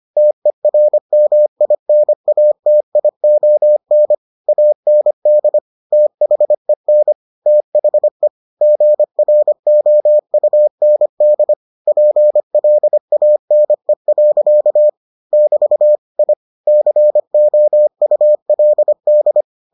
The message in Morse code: TERMINATION AND THEN THE GROUND PLANE. <BT> I COULD

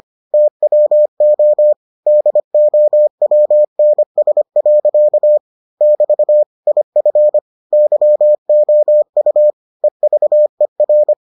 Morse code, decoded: TWO DOWNS. <BT> IF YOU EVER

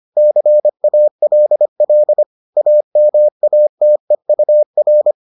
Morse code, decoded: CALL AMATEUR

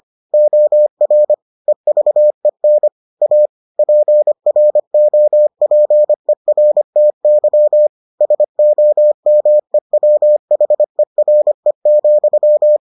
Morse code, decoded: OR EVEN A PROPERTY SOMEWHERE,

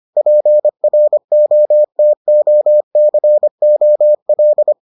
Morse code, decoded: PROTOCOL

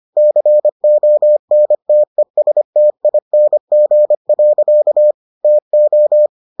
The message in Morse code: CONTESTING. TO

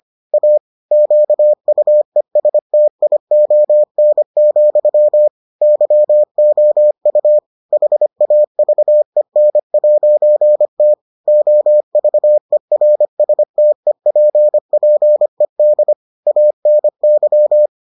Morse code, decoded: A QUESTION, YOU HAVEN'T OVERSTEPPED ANY

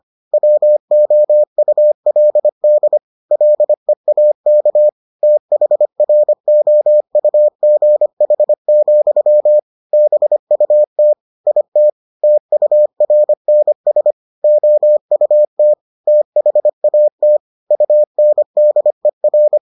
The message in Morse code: WOULD LEAK THROUGH, BUT IT TURNS OUT THAT UNDER

